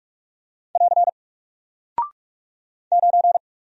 Morse code, heard as a keyed tone, about 44 wpm, 700 Hz. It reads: <AR> 9
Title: Single Character (Letter:Number:Punctuation:Prosign) Alt 45wpm